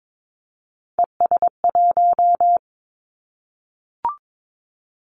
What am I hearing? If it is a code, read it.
Morse code, heard as a keyed tone, about 22 wpm, 700 Hz. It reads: ES1
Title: Callsign Prefixes for Top 50 DXCC - 22wpm